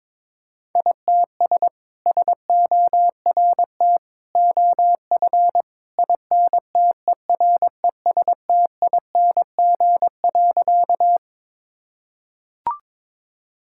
Morse code, heard as a keyed tone, about 22 wpm, 700 Hz. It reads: ITS SORT OF INTERESTING.
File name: practice / 22wpm / Sentences with 2-Word Phrases - 22wpm